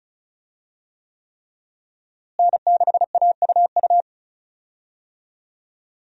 Morse code, heard as a keyed tone, about 36 wpm, 700 Hz. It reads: N6AUU
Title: USA Call Signs - 5 Characters - Repeated 3x - 35wpm